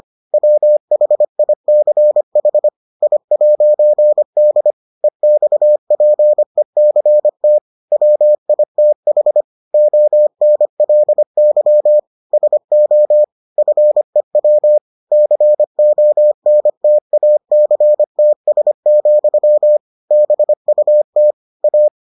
Morse code, heard as a keyed tone, about 25 wpm, 600 Hz. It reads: WHICH I'D EXPECT WITH ONLY SO FEW CONTACTS, BUT A